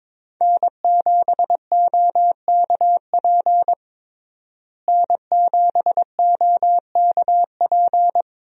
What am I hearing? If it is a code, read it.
Morse code, heard as a keyed tone, about 21 wpm, 700 Hz. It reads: N7OKP N7OKP
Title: USA Call Signs - 5 Characters - Repeated 2x - 22wpm